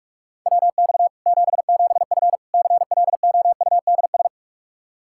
Morse code, heard as a keyed tone, about 46 wpm, 700 Hz. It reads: WX 77F CLOUDS
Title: Practice QSO - Exchange 2 - Info - 45wpm